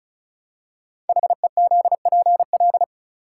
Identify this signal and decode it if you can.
Morse code, heard as a keyed tone, about 36 wpm, 700 Hz. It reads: HEZPL
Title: Code Groups - 5 Letters - 35wpm